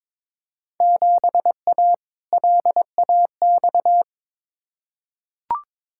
Morse code, heard as a keyed tone, about 22 wpm, 700 Hz. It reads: 7A LAX
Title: Field Day Exchanges Encore - 22wpm